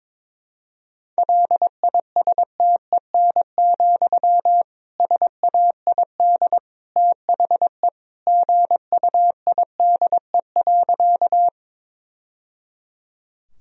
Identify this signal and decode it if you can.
Morse code, heard as a keyed tone, about 23 wpm, 700 Hz. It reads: LISTEN, SAID THE GUIDE.
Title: Sentences from Top 1000 Words - Review Part 3 - 22wpm